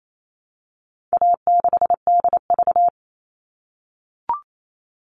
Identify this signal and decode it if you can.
Morse code, heard as a keyed tone, about 28 wpm, 700 Hz. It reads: A6DV
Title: All Cut Numbers - 4-Digit 28wpm